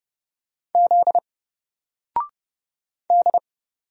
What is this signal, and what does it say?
Morse code, heard as a keyed tone, about 31 wpm, 700 Hz. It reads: Z D
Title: Single Letter-Number Alt 30wpm